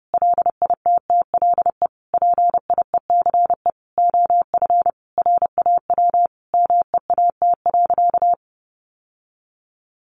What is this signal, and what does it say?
Morse code, heard as a keyed tone, about 30 wpm, 700 Hz. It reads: LITTLE PIECE OF RAW MEAT.
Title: Sentences from Top 1500 Words 30wpm